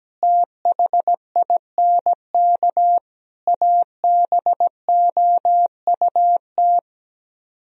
Morse code, heard as a keyed tone, about 17 wpm, 700 Hz. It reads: THINK ABOUT
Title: Speed-Racing - Sets of 5 Words 17wpm